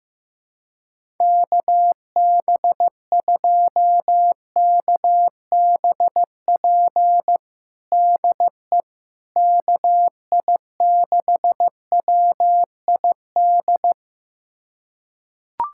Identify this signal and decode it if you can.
Morse code, heard as a keyed tone, about 15 wpm, 700 Hz. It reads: KB2KBP DE KI6WID
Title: USA Callsign DE Callsign - 6 Characters - 15wpm